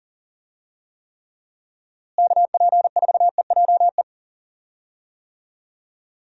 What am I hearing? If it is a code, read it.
Morse code, heard as a keyed tone, about 40 wpm, 700 Hz. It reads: KP4EJE
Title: DXCC-100 Call Signs - 6 Characters - Repeated 3x - 40wpm